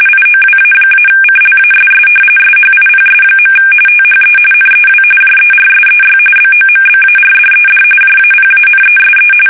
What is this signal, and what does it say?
RTTY, about 45.45 baud. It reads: IOWA-CLASS SHIPS. THE NEW JERSEY BOMBARDED JAPANESE
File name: fsk850 / battleshipnj